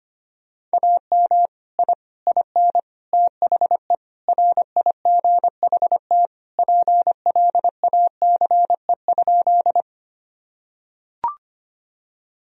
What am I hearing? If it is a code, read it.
Morse code, heard as a keyed tone, about 25 wpm, 700 Hz. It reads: AM I IN THE RIGHT PLACE?
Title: Sentences from Top 1000 Words - Review Part 2 - 25wpm